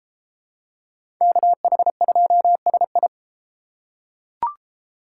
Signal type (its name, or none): Morse code